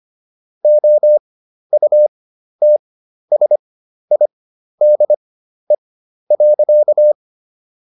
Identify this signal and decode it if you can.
Morse code, heard as a keyed tone, about 25 wpm, spaced as for 12 wpm, 600 Hz. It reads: OUTSIDE.